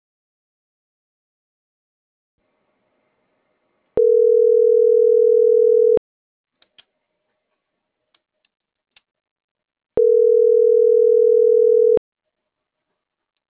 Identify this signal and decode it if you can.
Ringback tone (North America), 440 + 480 Hz.